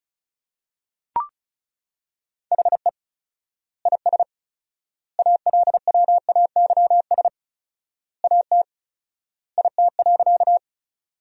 Morse code, heard as a keyed tone, about 36 wpm, 700 Hz. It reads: HE IS ALWAYS AT IT.
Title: 4x Word-Spacing - Sentences from Top 300 Words - 35wpm